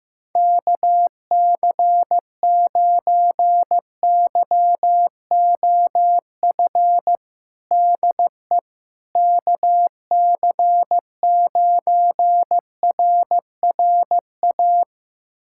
Morse code, heard as a keyed tone, about 15 wpm, 700 Hz. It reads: KC9YOF DE KC9RRA
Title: USA Callsign DE Callsign 15wpm